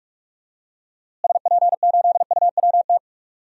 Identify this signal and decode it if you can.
Morse code, heard as a keyed tone, about 44 wpm, 700 Hz. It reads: SP8UWT
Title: DXCC-100 Call Signs - 6 Characters - Repeated 2x - 45wpm